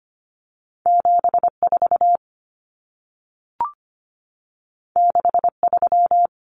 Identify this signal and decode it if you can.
Morse code, heard as a keyed tone, about 25 wpm, 700 Hz. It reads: 74 63
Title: Common Cut Numbers - 2-Digit 25wpm